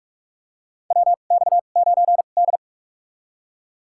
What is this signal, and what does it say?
Morse code, heard as a keyed tone, about 46 wpm, spaced as for 33 wpm, 700 Hz. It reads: WX9D
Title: USA Call Signs - 4 Characters - 2x Character Spacing - 45wpm